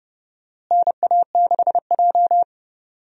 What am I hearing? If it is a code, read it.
Morse code, heard as a keyed tone, about 30 wpm, 700 Hz. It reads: NA6J
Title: USA Call Signs - 4 Characters - Repeated 3x - 30wpm